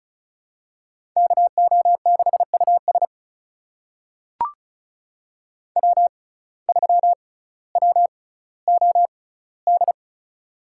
Morse code, heard as a keyed tone, about 36 wpm, spaced as for 18 wpm, 700 Hz. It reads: KO6US W3WOD
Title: USA Call Signs - 5 Characters - 6x Character Spacing - 35wpm